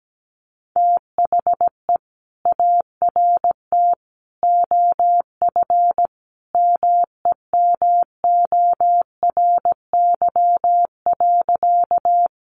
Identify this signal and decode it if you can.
Morse code, heard as a keyed tone, about 17 wpm, 700 Hz. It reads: THE ART OF MEMORY.
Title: Sentences from Top 1300 Words 17wpm